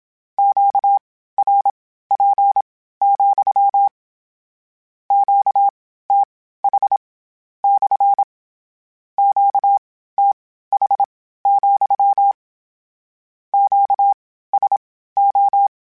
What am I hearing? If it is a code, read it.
Morse code, heard as a keyed tone, about 27 wpm, spaced as for 15 wpm, 800 Hz. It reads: QRP, QTH / QTH, QSO
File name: Q Calls / Q Calls-mp3s / Q Calls SPSL2 @25